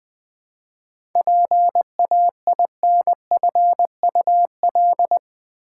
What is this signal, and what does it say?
Morse code, heard as a keyed tone, about 20 wpm, 690 Hz. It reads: PAINFUL
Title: 7 Letter Words - First 3 Letters Given - 20wpm